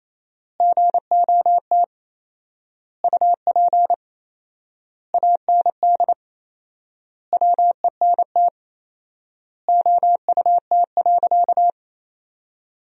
Morse code, heard as a keyed tone, about 27 wpm, 700 Hz. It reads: GOT UP AND WENT OUT.
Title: 4x Word-Spacing - Sentences from Top 300 Words - 28wpm